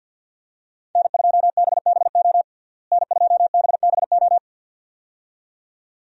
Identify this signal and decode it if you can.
Morse code, heard as a keyed tone, about 52 wpm, 700 Hz. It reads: N2BBO N2BBO
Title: USA Call Signs - 5 Characters - Repeated 2x - 50wpm